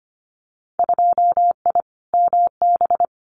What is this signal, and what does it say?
Morse code, heard as a keyed tone, about 25 wpm, 700 Hz. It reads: 2I MB
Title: Winter Field Day Exchanges Encore 25wpm